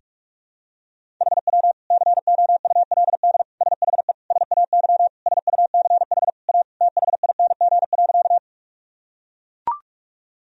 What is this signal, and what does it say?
Morse code, heard as a keyed tone, about 42 wpm, 700 Hz. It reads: HW COULD SHE SAY SUCH A THING.
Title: Sentences from Top 200 Words and Abbreviations 45wpm